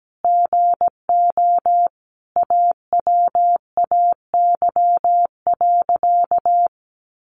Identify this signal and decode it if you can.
Morse code, heard as a keyed tone, about 17 wpm, 700 Hz. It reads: GO AWAY.
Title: Sentences from Top 1500 Words 17wpm